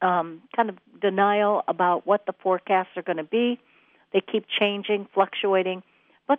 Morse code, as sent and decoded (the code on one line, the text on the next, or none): none